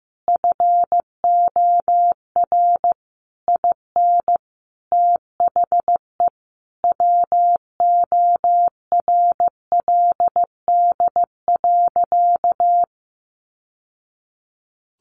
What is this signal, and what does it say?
Morse code, heard as a keyed tone, about 15 wpm, 700 Hz. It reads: FOR IN THE WORLD.
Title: Sentences from Top 1100 Words 15wpm